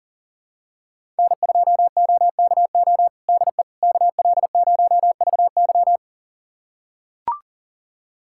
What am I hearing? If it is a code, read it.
Morse code, heard as a keyed tone, about 39 wpm, 700 Hz. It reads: N2OKO DE KL0VY
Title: USA Callsign DE Callsign 40wpm